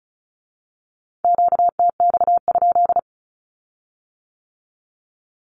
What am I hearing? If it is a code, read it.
Morse code, heard as a keyed tone, about 36 wpm, 700 Hz. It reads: QTX?
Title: Q-Codes with Spoken Meaning - 35wpm